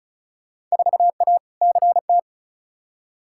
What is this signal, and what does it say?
Morse code, heard as a keyed tone, about 36 wpm, 700 Hz. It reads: 4A CT
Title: Field Day Exchanges Encore - 35wpm